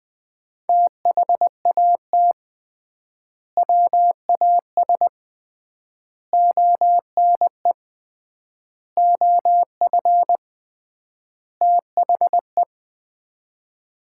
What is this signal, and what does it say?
Morse code, heard as a keyed tone, about 21 wpm, 700 Hz. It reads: THAT WAS ONE OF THE
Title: 3x Word-Spacing - Sets of 5 Words - 20wpm